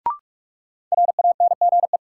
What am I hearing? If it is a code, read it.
Morse code, heard as a keyed tone, about 43 wpm, 700 Hz. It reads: RANGE
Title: Sets of 4 Words - New Word Drill - 45wpm